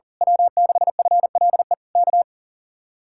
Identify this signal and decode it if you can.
Morse code, heard as a keyed tone, about 38 wpm, 700 Hz. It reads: W6FLE K
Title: Speed-Racing Common QSO Phrases - 40wpm